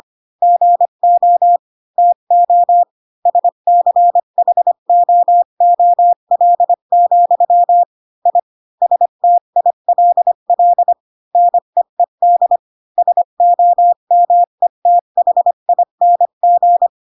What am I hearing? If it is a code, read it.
Morse code, heard as a keyed tone, about 25 wpm, 700 Hz. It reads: GO TO SCHOOL, I STILL NEED SOMETHING
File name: Pinocchio / Pinocchio ch8 20 25 ik1hge